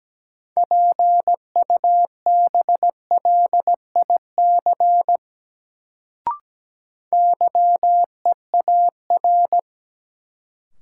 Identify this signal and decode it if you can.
Morse code, heard as a keyed tone, about 17 wpm, 700 Hz. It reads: PUBLIC YEAR
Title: Sets of 2 Words - New Word Drill - 17wpm